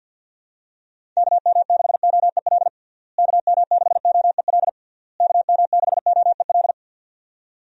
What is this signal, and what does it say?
Morse code, heard as a keyed tone, about 52 wpm, 700 Hz. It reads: KM6OEL KM6OEL KM6OEL
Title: USA Call Signs - 6 Characters - Repeated 3x - 50wpm